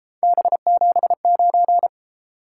Morse code, heard as a keyed tone, about 33 wpm, 700 Hz. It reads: B79